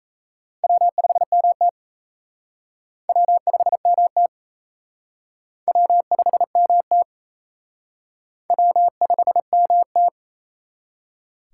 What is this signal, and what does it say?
Morse code, heard as a keyed tone, about 37 wpm, 700 Hz. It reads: W5MT W5MT W5MT W5MT